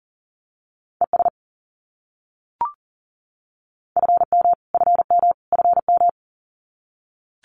Morse code, heard as a keyed tone, about 40 wpm, 700 Hz. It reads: ES FM FM FM
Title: Abbreviations Repeated 3x - 40wpm